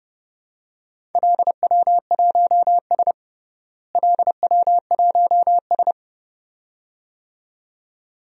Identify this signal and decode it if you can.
Morse code, heard as a keyed tone, about 29 wpm, 700 Hz. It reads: LW1S LW1S
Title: DXCC-100 Call Signs - 4 Characters - Repeated 2x - 30wpm